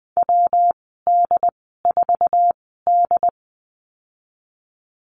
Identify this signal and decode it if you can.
Morse code, heard as a keyed tone, about 20 wpm, spaced as for 15 wpm, 700 Hz. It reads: WD4D
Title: 128 Lesson 21g - Number 6 - Call Signs 20wpm